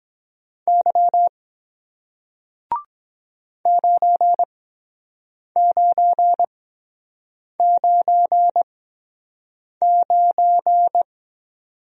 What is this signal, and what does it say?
Morse code, heard as a keyed tone, about 22 wpm, spaced as for 7 wpm, 700 Hz. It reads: Y 9999